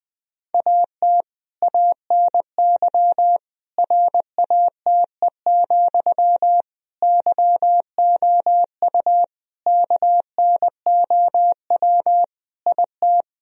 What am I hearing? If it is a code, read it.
Morse code, heard as a keyed tone, about 21 wpm, 700 Hz. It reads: AT ANY RATE, YOU KNOW IT
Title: Sentences from Top 1000 Words - Review Part 4 - 20wpm